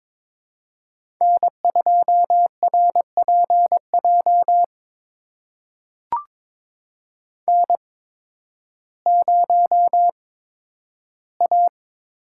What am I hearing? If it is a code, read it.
Morse code, heard as a keyed tone, about 22 wpm, spaced as for 12 wpm, 700 Hz. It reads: N2RPJ N0A
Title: USA Call Signs - 5 Characters - 8x Character Spacing - 22wpm